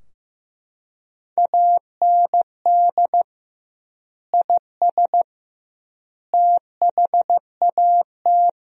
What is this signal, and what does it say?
Morse code, heard as a keyed tone, about 15 wpm, 700 Hz. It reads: AND IS THAT
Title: 2x Word-Spacing - Sentences from Top 100 Words - 15wpm